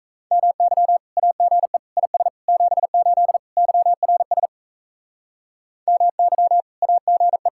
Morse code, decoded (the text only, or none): MY AGE IS 78 YRS MY AGE